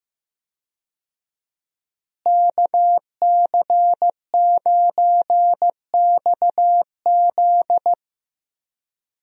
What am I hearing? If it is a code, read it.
Morse code, heard as a keyed tone, about 15 wpm, 700 Hz. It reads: KC9XZ
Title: USA Call Signs - 5 Characters - Repeated 2x - 15wpm